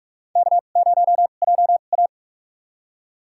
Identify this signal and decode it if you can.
Morse code, heard as a keyed tone, about 48 wpm, spaced as for 33 wpm, 700 Hz. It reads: K0JA
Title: USA Call Signs - 4 Characters - 2x Character Spacing - 45wpm